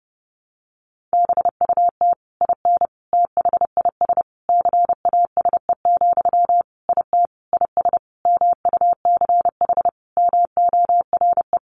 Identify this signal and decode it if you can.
Morse code, heard as a keyed tone, about 30 wpm, 700 Hz. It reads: BUT IN THIS CASE, IT IS MUCH MORE
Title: Sentences from Top 1200 Words 30wpm